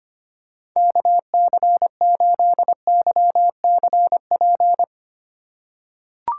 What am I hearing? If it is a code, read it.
Morse code, heard as a keyed tone, about 25 wpm, 700 Hz. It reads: KC8YCP
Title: USA Call Signs - 6 Characters - 2x Character Spacing - 25wpm